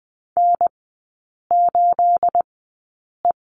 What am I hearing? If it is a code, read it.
Morse code, heard as a keyed tone, about 20 wpm, spaced as for 8 wpm, 690 Hz. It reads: N8E
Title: 212 Lesson 34e - Number 8 - Three Characters 20wpm